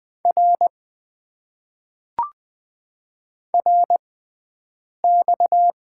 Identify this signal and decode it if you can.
Morse code, heard as a keyed tone, about 20 wpm, spaced as for 7 wpm, 700 Hz. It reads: R RX
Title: Top 90 QSO Elements - 6x Character Spacing - 20wpm